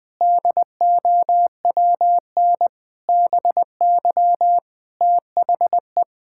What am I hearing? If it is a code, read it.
Morse code, heard as a keyed tone, about 20 wpm, 700 Hz. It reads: DOWN BY THE
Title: Sentences from Top 1000 Words - Review Part 5 - 20wpm